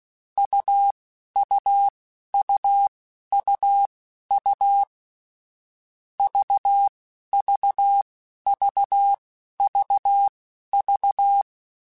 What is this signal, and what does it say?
Morse code, heard as a keyed tone, about 16 wpm, spaced as for 11 wpm, 800 Hz. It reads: UUUUU VVVVV
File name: Difficult Combos / Difficult Combos-mp3s / AUV AUV4 @10